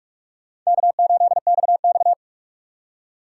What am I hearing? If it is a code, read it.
Morse code, heard as a keyed tone, about 41 wpm, 700 Hz. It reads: K8XX
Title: USA Call Signs - 2x Character Spacing - 45wpm